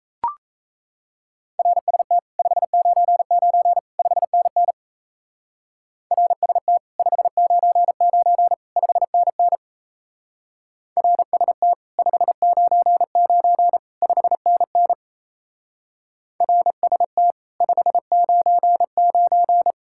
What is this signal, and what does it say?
Morse code, heard as a keyed tone, about 35 wpm, 700 Hz. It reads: RST 599 5NN RST 599 5NN RST 599 5NN RST 599
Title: Speed-Racing Common QSO Phrases - 28wpm